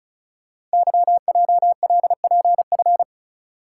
Morse code, heard as a keyed tone, about 36 wpm, 700 Hz. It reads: YJLPF
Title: Code Groups - 5 Letters - 35wpm